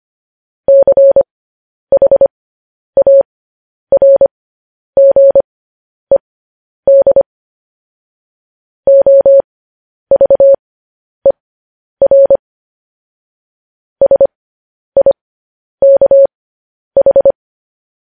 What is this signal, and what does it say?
Morse code, heard as a keyed tone, about 25 wpm, spaced as for 10 wpm, 550 Hz. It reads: CHARGED OVER SIKH